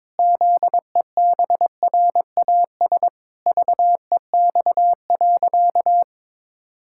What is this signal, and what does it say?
Morse code, heard as a keyed tone, about 22 wpm, 700 Hz. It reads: ZEBRAS VEX.
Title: Pangram Challenge v2 - 22wpm